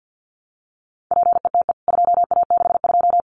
Morse code, heard as a keyed tone, about 50 wpm, 700 Hz. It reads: PETE 2A62
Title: CWT Exchange 50wpm